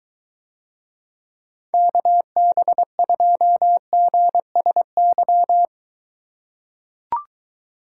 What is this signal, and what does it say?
Morse code, heard as a keyed tone, about 23 wpm, 700 Hz. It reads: KB2GSY